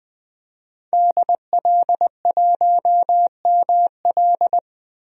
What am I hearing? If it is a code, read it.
Morse code, heard as a keyed tone, about 20 wpm, 700 Hz. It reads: DL1ML
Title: DXCC-100 Call Signs - 5 Characters - 20wpm